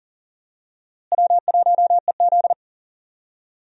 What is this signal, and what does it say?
Morse code, heard as a keyed tone, about 37 wpm, 700 Hz. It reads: W1EZ
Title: USA Call Signs - 4 Characters - Repeated 2x - 40wpm